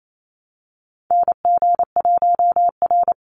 Morse code, heard as a keyed tone, about 28 wpm, 700 Hz. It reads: NG1R
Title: Call Signs Repeated 2x - 28wpm